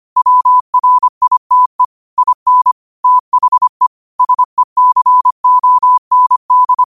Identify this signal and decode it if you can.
Morse code, heard as a keyed tone, about 25 wpm, 1000 Hz. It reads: WRITE IN THE SECOND
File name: qotdfiles / 25wpm